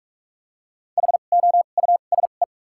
Morse code, heard as a keyed tone, about 46 wpm, spaced as for 33 wpm, 700 Hz. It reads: HOUSE
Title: 2x Character Spacing - Top 200 Words - 45wpm